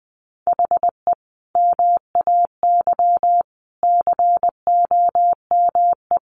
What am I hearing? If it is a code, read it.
Morse code, heard as a keyed tone, about 20 wpm, 700 Hz. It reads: HE MAY COME
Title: Sentences from Top 300 Words 20wpm